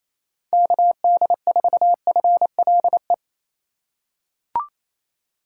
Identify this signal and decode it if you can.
Morse code, heard as a keyed tone, about 28 wpm, 700 Hz. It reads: KD4FLE
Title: USA Call Signs - 6 Characters - 3x Character Spacing - 28wpm